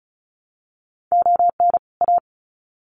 Morse code, heard as a keyed tone, about 34 wpm, 700 Hz. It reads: ON A
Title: Sets of 2 Words 35wpm